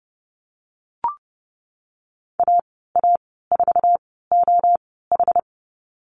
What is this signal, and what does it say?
Morse code, heard as a keyed tone, about 30 wpm, spaced as for 17 wpm, 700 Hz. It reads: AA4OH
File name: Lesson 14g - Number 2 - Call Signs 30wpm